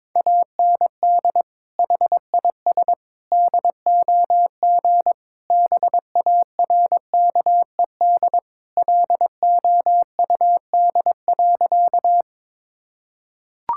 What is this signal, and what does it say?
Morse code, heard as a keyed tone, about 21 wpm, 700 Hz. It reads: AND HIS DOG BARKED LOUD.
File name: practice / 22wpm / Sentences with 2-Word Phrases - 22wpm